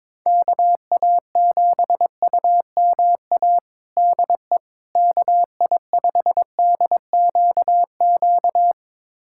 Morse code, heard as a keyed tone, about 22 wpm, 700 Hz. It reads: KA7UMA DE KI5DQQ
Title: USA Callsign DE Callsign - 6 Characters - 22wpm